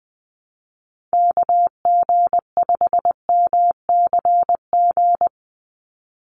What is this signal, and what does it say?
Morse code, heard as a keyed tone, about 20 wpm, 700 Hz. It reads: KG5MCG
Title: USA Call Signs 20wpm